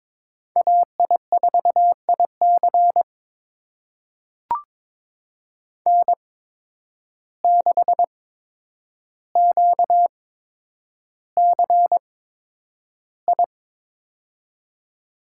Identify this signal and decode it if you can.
Morse code, heard as a keyed tone, about 22 wpm, spaced as for 9 wpm, 700 Hz. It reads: AI4IC N6QCI
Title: USA Call Signs - 5 Characters - 8x Character Spacing - 22wpm